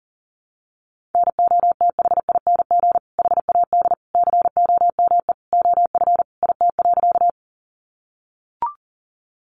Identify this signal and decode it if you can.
Morse code, heard as a keyed tone, about 40 wpm, 700 Hz. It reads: NOTHING HAD COME OF IT.
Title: Sentences from Top 500 Words 40wpm